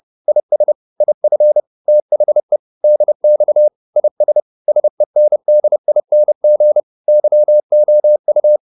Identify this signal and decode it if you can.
Morse code, heard as a keyed tone, about 30 wpm, 600 Hz. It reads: IS IF THE DX IS SENDING YOU